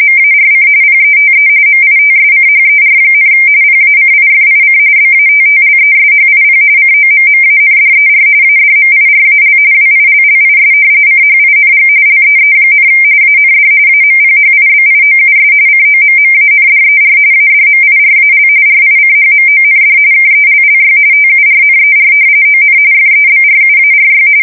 RTTY, about 45.45 baud. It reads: MARC PETE MITSCHER, PACIFIC BATTLESHIP COMMANDER ADMIRAL WILLIS CHING LEE, AND AMPHIBIOUS COMMANDER ADMIRAL RICHMOND KELLY TURNER ON FEBRUARY
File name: fsk170 / battleshipnj